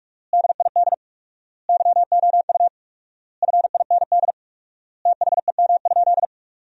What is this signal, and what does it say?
Morse code, heard as a keyed tone, about 46 wpm, 700 Hz. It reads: DID YOU FIND THEM?
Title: 4x Word-Spacing - Sentences from Top 200 Words - 45wpm